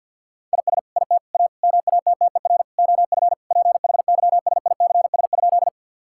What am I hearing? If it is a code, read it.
Morse code, heard as a keyed tone, about 50 wpm, 700 Hz. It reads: IS IT A MATTER OF PHYSICS?
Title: Sentences from Top 1000 Words - Review Part 5 - 50wpm